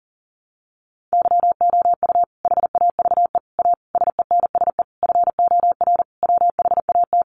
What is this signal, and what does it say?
Morse code, heard as a keyed tone, about 40 wpm, 700 Hz. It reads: YOU HAVE A SENSE FOR WHAT
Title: Sentences from Top 700 Words 40wpm